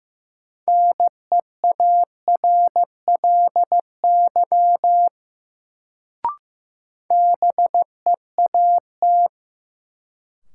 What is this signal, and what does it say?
Morse code, heard as a keyed tone, about 15 wpm, 700 Hz. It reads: NEARLY BEAT
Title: Sentences from Top 1300 Words - New Word Drill - 15wpm